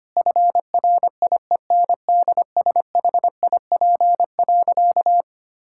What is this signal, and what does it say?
Morse code, heard as a keyed tone, about 25 wpm, 700 Hz. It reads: FRIENDSHIP.
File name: practice / 25wpm / Difficult Sentences with Hints - 25wpm